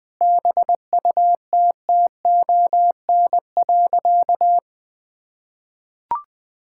Morse code, heard as a keyed tone, about 20 wpm, 700 Hz. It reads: BUTTON.
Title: Sentences with 6-Word Phrases - Spoken Beginning - 20wpm